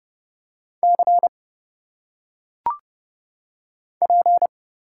Morse code, heard as a keyed tone, about 30 wpm, 700 Hz. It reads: C P
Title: Single Letters - Phonetic Pronunciation 30wpm